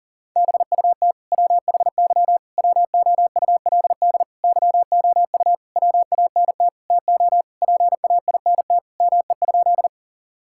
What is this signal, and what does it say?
Morse code, heard as a keyed tone, about 40 wpm, 700 Hz. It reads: BUT WHY WOULD YOU WANT TO PAINT ME?
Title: Sentences from Top 1000 Words - Review Part 6 - 40wpm